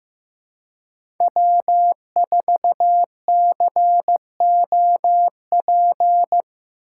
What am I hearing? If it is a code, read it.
Morse code, heard as a keyed tone, about 15 wpm, 700 Hz. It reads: W4COP
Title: USA Call Signs - 5 Characters - 3x Character Spacing - 15wpm